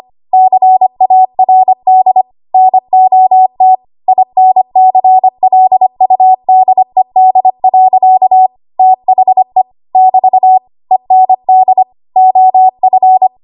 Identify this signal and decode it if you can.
Morse code, heard as a keyed tone, about 25 wpm, 750 Hz. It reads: CARD NOT INCLUDED. THE <BT> END OF
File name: Archive / 25 WPM / 25WPM